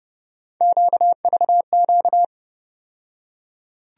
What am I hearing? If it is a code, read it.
Morse code, heard as a keyed tone, about 30 wpm, 700 Hz. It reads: QVQ
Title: USA Call Sign Postfixes - 3 Letters 30wpm